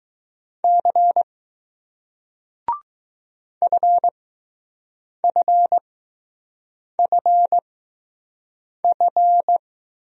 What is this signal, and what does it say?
Morse code, heard as a keyed tone, about 20 wpm, spaced as for 6 wpm, 700 Hz. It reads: C FFFF